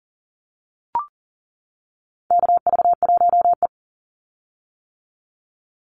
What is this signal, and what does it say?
Morse code, heard as a keyed tone, about 40 wpm, 700 Hz. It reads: KV1E
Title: Call Signs - 40wpm